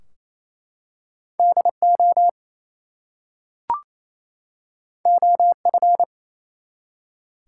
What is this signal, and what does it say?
Morse code, heard as a keyed tone, about 27 wpm, 700 Hz. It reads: DO OF
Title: Two Letter Words - 28wpm